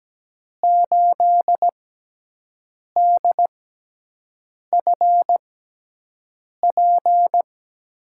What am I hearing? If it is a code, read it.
Morse code, heard as a keyed tone, about 17 wpm, spaced as for 6 wpm, 700 Hz. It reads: 8DFP
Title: USA Call Signs - 6 Characters - 6x Character Spacing - 17wpm